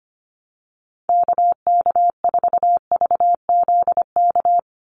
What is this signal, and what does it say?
Morse code, heard as a keyed tone, about 25 wpm, 700 Hz. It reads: KK4VZK
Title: USA Call Signs 25wpm